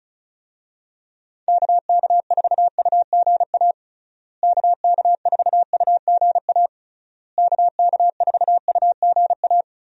Morse code, heard as a keyed tone, about 36 wpm, 700 Hz. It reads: KK4UGA KK4UGA KK4UGA
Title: USA Call Signs - 6 Characters - Repeated 3x - 35wpm